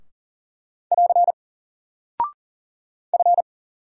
Morse code, heard as a keyed tone, about 40 wpm, 700 Hz. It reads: <AR> F
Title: Single Character (Letter:Number:Punctuation:Prosign) 40wpm